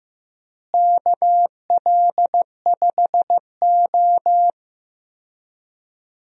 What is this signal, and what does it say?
Morse code, heard as a keyed tone, about 15 wpm, 700 Hz. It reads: KL5O
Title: USA Call Signs - 4 Characters - Repeated 3x - 15wpm